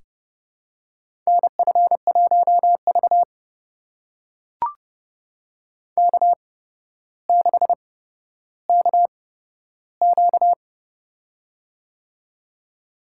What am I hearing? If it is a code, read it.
Morse code, heard as a keyed tone, about 30 wpm, spaced as for 13 wpm, 700 Hz. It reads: NF1V K6KQ
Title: USA Call Signs - 4 Characters - 8x Character Spacing - 30wpm